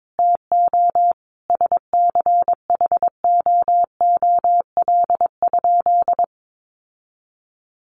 Morse code, heard as a keyed tone, about 22 wpm, 700 Hz. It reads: TO SCHOOL?
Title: Sentences from Top 1400 Words 22wpm